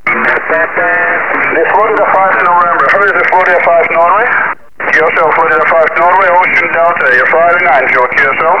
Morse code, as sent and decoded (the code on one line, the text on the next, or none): none